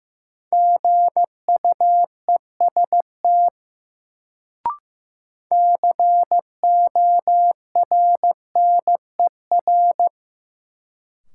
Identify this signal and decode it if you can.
Morse code, heard as a keyed tone, about 15 wpm, 700 Hz. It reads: GUEST CORNER
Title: Sentences from Top 1600 Words - New Word Drill Alt - 15wpm